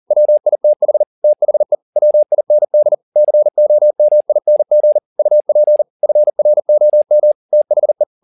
Morse code, heard as a keyed tone, about 40 wpm, 600 Hz. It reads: WITH THE WIND COMING UP FROM THE